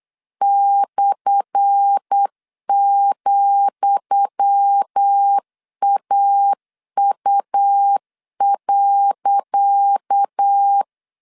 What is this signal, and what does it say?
Morse code, heard as a keyed tone, about 8 wpm, 800 Hz. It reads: /,AU.
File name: LEZIONI / 10 SIGNS / 10D COMA 8wpm 3749